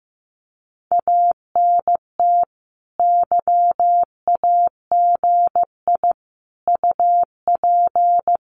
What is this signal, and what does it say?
Morse code, heard as a keyed tone, about 15 wpm, 700 Hz. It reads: ANT YAGI UP
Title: Common QSO Phrases Encore 15wpm